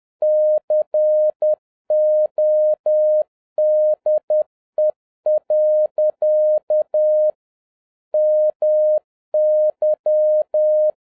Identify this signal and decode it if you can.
Morse code, heard as a keyed tone, about 10 wpm, 610 Hz. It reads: CODE. MY